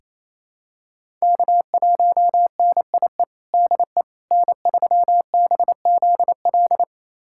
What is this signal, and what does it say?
Morse code, heard as a keyed tone, about 28 wpm, 700 Hz. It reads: K1NIE DE N3BZL
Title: USA Callsign DE Callsign - 5 Characters - 28wpm